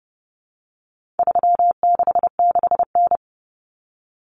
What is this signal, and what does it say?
Morse code, heard as a keyed tone, about 30 wpm, 700 Hz. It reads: 366N
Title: Common Cut Numbers - 4-Digit 30wpm